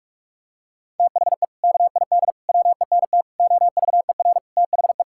Morse code, heard as a keyed tone, about 46 wpm, 700 Hz. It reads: THE KID WENT OVER THE
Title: Sentences from Top 1600 Words 45wpm